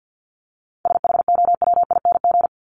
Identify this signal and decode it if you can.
Morse code, heard as a keyed tone, about 50 wpm, 700 Hz. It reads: SHOWING
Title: ING Suffix - Top 100 Words with Base Words 50wpm